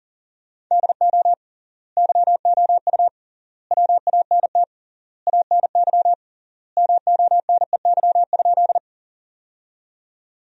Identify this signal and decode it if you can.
Morse code, heard as a keyed tone, about 40 wpm, 700 Hz. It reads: DO YOU WANT ANY MONEY?
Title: 3x Word-Spacing - Sentences from Top 400 Words - 40wpm